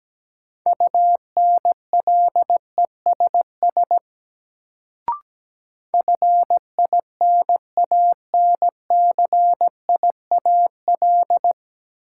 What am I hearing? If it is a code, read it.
Morse code, heard as a keyed tone, about 17 wpm, 700 Hz. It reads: UNLESS FINANCIAL
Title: Sentences from Top 1000 Words - Review Part 5 - New Word Drill - 17wpm